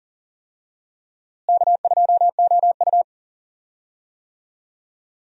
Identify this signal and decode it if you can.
Morse code, heard as a keyed tone, about 40 wpm, 700 Hz. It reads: K2OU
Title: USA Call Signs - 4 Characters - Repeated 3x - 40wpm